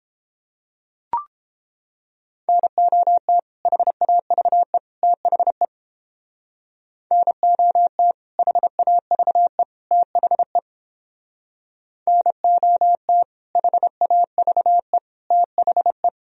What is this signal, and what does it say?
Morse code, heard as a keyed tone, about 30 wpm, 700 Hz. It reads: NOT HAVE THE NOT HAVE THE NOT HAVE THE